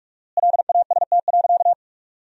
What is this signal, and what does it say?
Morse code, heard as a keyed tone, about 46 wpm, 700 Hz. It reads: LAST.
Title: 4x Word-Spacing - Sentences from Top 300 Words - 45wpm